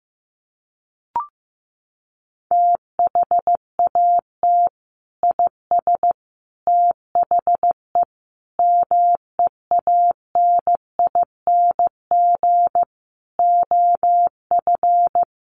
Morse code, heard as a keyed tone, about 15 wpm, 700 Hz. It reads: THAT IS THE MEANING OF
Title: Sentences from Top 900 Words 15wpm